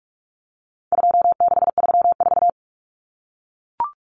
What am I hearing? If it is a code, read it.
Morse code, heard as a keyed tone, about 44 wpm, 700 Hz. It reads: 2634
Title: Numbers - 4-Digit 45wpm